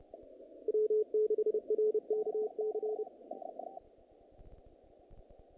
Morse code, heard as a keyed tone, about 30 wpm, 430 Hz. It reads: W6RKC
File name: contests / naqp-cw-k5tr-op-k5pi- -2100